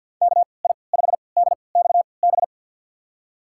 Morse code, heard as a keyed tone, about 50 wpm, spaced as for 28 wpm, 700 Hz. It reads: KI5DXB
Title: USA Call Signs - 6 Characters - 3x Character Spacing - 50wpm